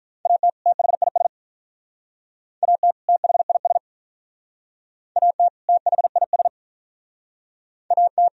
Morse code, heard as a keyed tone, about 51 wpm, 700 Hz. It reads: AT THIS AT THIS AT THIS AT